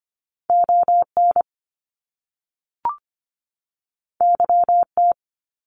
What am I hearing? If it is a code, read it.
Morse code, heard as a keyed tone, about 25 wpm, 700 Hz. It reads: ON YT
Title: Canadian Provinces and Territories 25wpm